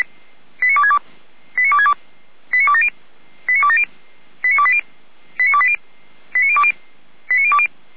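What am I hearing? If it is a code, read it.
Five-tone selective call (ZVEI), heard as quick runs of five tones, tones of about 80 ms. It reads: 89262 89262 89270 89270 89280 89280 89020 89020
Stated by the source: ZVEI five-tone selective call, address 89358, received by SDR.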